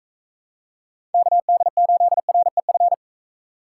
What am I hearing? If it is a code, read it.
Morse code, heard as a keyed tone, about 42 wpm, 700 Hz. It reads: KD8REF